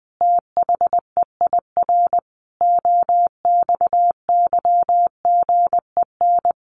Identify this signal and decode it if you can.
Morse code, heard as a keyed tone, about 20 wpm, 700 Hz. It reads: THEIR OXYGEN
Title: A Sound Of Thunder 20wpm